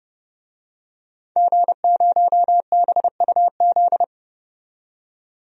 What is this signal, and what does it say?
Morse code, heard as a keyed tone, about 29 wpm, 700 Hz. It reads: G0BUZ
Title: DXCC-100 Call Signs - 5 Characters - 30wpm